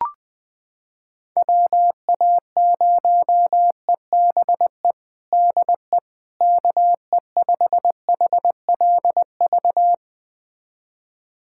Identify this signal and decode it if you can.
Morse code, heard as a keyed tone, about 20 wpm, 700 Hz. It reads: WA0EBE DE KE5HLV
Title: USA Callsign DE Callsign - 6 Characters - 20wpm